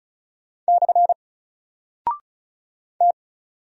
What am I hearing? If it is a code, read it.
Morse code, heard as a keyed tone, about 34 wpm, 700 Hz. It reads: / T
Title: Single Character (Letter:Number:Punctuation:Prosign) Alt 35wpm